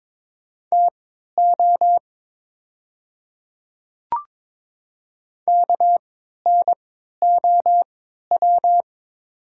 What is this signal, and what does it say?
Morse code, heard as a keyed tone, about 23 wpm, spaced as for 13 wpm, 700 Hz. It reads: TO KNOW
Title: 3x Character Spacing - Top 200 Words - 22wpm